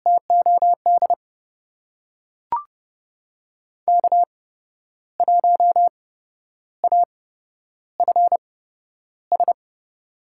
Morse code, heard as a keyed tone, about 30 wpm, spaced as for 11 wpm, 700 Hz. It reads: TOD K1AFS
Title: USA Call Signs - 5 Characters - 8x Character Spacing - 30wpm